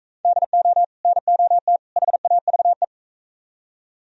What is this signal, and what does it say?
Morse code, heard as a keyed tone, about 41 wpm, 700 Hz. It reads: DO NOT HAVE